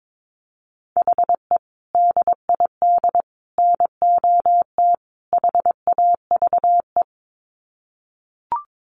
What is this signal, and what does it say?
Morse code, heard as a keyed tone, about 22 wpm, 700 Hz. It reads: HE DID NOT HAVE
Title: Sets of 4 Words 22wpm